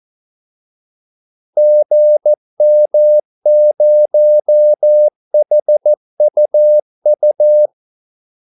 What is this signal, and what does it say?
Morse code, heard as a keyed tone, about 14 wpm, 600 Hz. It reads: GM0HUU
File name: Callsigns-14wpm